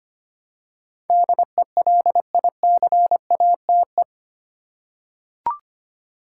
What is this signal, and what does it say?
Morse code, heard as a keyed tone, about 25 wpm, 700 Hz. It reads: DELICATE
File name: 8 Letter Words - First 3 Letters Given - 25wpm